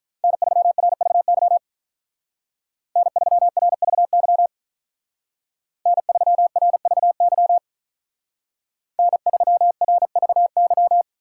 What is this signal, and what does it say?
Morse code, heard as a keyed tone, about 43 wpm, 700 Hz. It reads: N3RVY N3RVY N3RVY N3RVY
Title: Speed Racing - USA Call Signs 35wpm